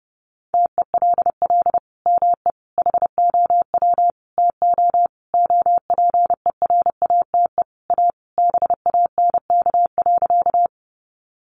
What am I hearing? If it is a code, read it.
Morse code, heard as a keyed tone, about 30 wpm, 700 Hz. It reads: TELL ME HOW TO OPERATE A BANK.
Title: Sentences from Top 1500 Words 30wpm